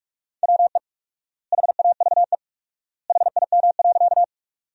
Morse code, heard as a keyed tone, about 46 wpm, 700 Hz. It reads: WE HAVE HIM.
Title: 4x Word-Spacing - Sentences from Top 100 Words - 45wpm